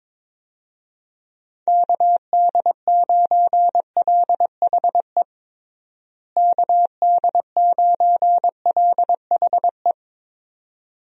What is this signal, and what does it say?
Morse code, heard as a keyed tone, about 22 wpm, 700 Hz. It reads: KD9LHE KD9LHE
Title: USA Call Signs - 6 Characters - Repeated 2x - 22wpm